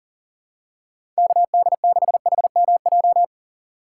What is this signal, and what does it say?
Morse code, heard as a keyed tone, about 40 wpm, 700 Hz. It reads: KD6HMJ
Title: USA Call Signs - 6 Characters - 3x Character Spacing - 40wpm